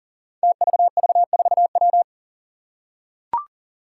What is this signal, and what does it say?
Morse code, heard as a keyed tone, about 43 wpm, 700 Hz. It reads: TVV4W
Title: Five Challenging Letter-Number Combinations 40wpm